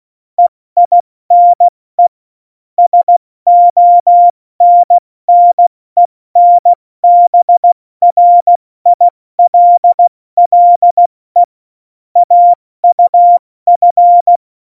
Morse code, heard as a keyed tone, about 16 wpm, 700 Hz. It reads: EINE SONNENBRILLE AUF
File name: Morsekurs / Witze / Witz 22. Mai 2018, 70 F80